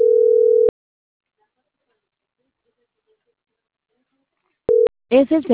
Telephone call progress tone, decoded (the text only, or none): ringback tone